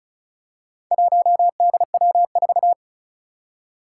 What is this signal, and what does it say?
Morse code, heard as a keyed tone, about 36 wpm, 700 Hz. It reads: 1DW4
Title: Four Challenging Letter-Number Combinations 35wpm